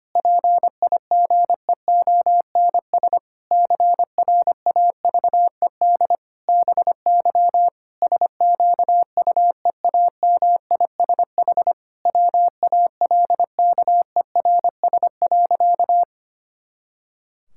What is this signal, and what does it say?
Morse code, heard as a keyed tone, about 25 wpm, 700 Hz. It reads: PIGEONS CRAVED BY SQUEAMISH WALKERS.
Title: Pangram Challenge v2 - 25wpm